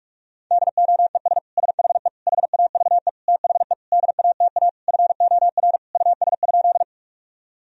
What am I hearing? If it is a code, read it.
Morse code, heard as a keyed tone, about 41 wpm, 700 Hz. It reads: DOES SHE HAVE THE DATA FOR US?
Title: Sentences from Top 1000 Words - Review Part 3 - 45wpm